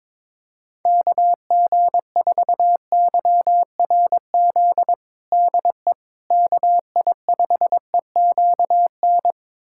Morse code, heard as a keyed tone, about 22 wpm, 700 Hz. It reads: KG4YRZ DE KI5EQN
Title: USA Callsign DE Callsign - 6 Characters - 22wpm